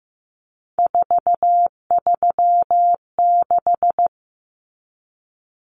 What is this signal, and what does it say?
Morse code, heard as a keyed tone, about 15 wpm, 700 Hz. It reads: 436
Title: Numbers - 3-Digit 15wpm